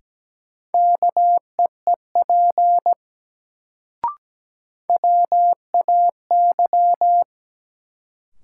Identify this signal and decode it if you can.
Morse code, heard as a keyed tone, about 17 wpm, 700 Hz. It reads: KEEP WAY
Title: Sets of 2 Words - New Word Drill - 17wpm